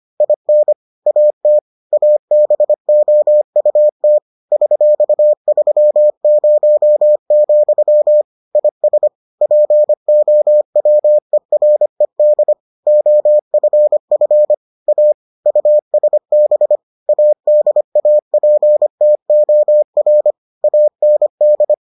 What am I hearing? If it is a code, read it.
Morse code, heard as a keyed tone, about 25 wpm, 600 Hz. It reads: IN AT ABOUT $30, IS POWERED OFF A USB ADAPTOR AND